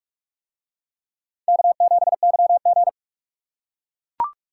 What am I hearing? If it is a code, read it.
Morse code, heard as a keyed tone, about 43 wpm, 700 Hz. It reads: K7YG